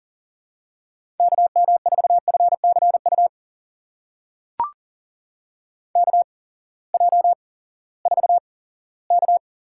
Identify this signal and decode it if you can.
Morse code, heard as a keyed tone, about 40 wpm, spaced as for 20 wpm, 700 Hz. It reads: KM4FCU KJ4K
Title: USA Call Signs - 6 Characters - 8x Character Spacing - 40wpm